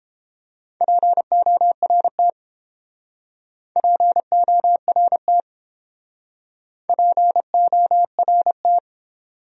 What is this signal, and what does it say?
Morse code, heard as a keyed tone, about 29 wpm, 700 Hz. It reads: PORT PORT PORT